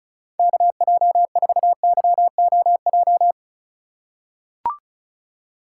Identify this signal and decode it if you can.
Morse code, heard as a keyed tone, about 36 wpm, 700 Hz. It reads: KJ4YOJ
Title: USA Call Signs - 6 Characters - 4x Character Spacing - 35wpm